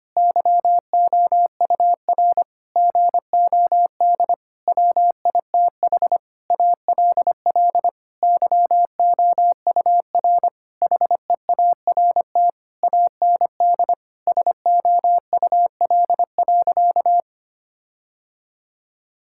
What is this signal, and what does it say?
Morse code, heard as a keyed tone, about 25 wpm, 700 Hz. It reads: YOUR GOD WITH ALL YOUR HEART AND SOUL.
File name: practice / 25wpm / Sentences with 9-Word Phrases - 25wpm